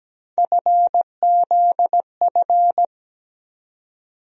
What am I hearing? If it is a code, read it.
Morse code, heard as a keyed tone, about 17 wpm, 700 Hz. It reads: FZF
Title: USA Call Sign Postfixes - 3 Letters 17wpm